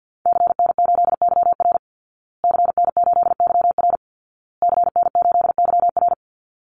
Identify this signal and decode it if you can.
Morse code, heard as a keyed tone, about 49 wpm, 700 Hz. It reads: CN8YR CN8YR CN8YR
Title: Call Signs Repeated 2x - 50wpm